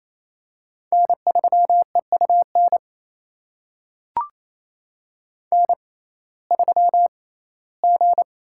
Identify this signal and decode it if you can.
Morse code, heard as a keyed tone, about 28 wpm, spaced as for 17 wpm, 700 Hz. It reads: N3EUN N3G
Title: USA Call Signs - 5 Characters - 6x Character Spacing - 28wpm